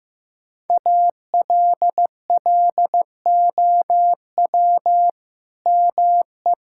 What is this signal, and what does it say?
Morse code, heard as a keyed tone, about 15 wpm, 700 Hz. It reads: ALLOW ME
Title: Sentences from Top 1000 Words - Review Part 5 - 15wpm